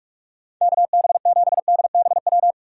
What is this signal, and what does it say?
Morse code, heard as a keyed tone, about 44 wpm, 700 Hz. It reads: KB7DBW
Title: USA Call Signs - 6 Characters - 45wpm